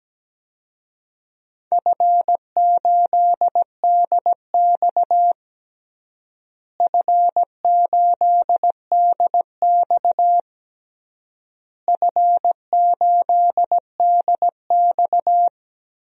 Morse code, heard as a keyed tone, about 17 wpm, 700 Hz. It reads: F8DX F8DX F8DX
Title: DXCC-100 Call Signs - 4 Characters - Repeated 3x - 17wpm